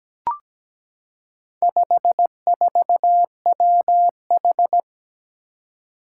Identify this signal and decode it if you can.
Morse code, heard as a keyed tone, about 17 wpm, 700 Hz. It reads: 54WH
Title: Four Challenging Letter-Number Combinations 17wpm